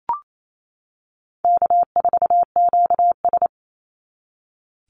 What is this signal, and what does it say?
Morse code, heard as a keyed tone, about 28 wpm, 700 Hz. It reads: K4QS
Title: NA SOTA Chaser Callsigns 28wpm